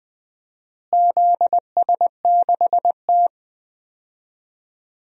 Morse code, heard as a keyed tone, about 20 wpm, 700 Hz. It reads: ZS6T
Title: DXCC-100 Call Signs - 4 Characters - Repeated 3x - 20wpm